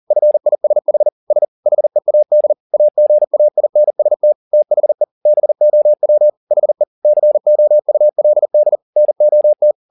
Morse code, heard as a keyed tone, about 40 wpm, 600 Hz. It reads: FISH S HEAD AGAINST THE BOW HE COULD NOT
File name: the old man and the sea 2 40wpm